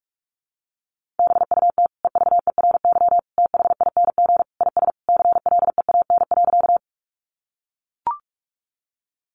Morse code, heard as a keyed tone, about 44 wpm, 700 Hz. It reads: BUT EVERYTHING IS CLEAN.
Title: Sentences from Top 1100 Words 45wpm